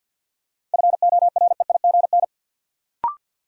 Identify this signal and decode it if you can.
Morse code, heard as a keyed tone, about 53 wpm, 700 Hz. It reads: FOREIGN
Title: Sentences from Top 1400 Words - New Word Drill Alt - 50wpm